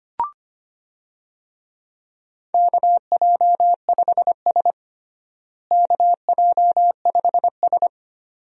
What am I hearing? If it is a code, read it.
Morse code, heard as a keyed tone, about 25 wpm, 700 Hz. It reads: KJ5S KJ5S
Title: USA Call Signs - 4 Characters - Repeated 2x - 25wpm